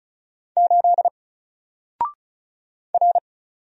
Morse code, heard as a keyed tone, about 36 wpm, 700 Hz. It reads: 8 R
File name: Single-Letter-Number-35wpm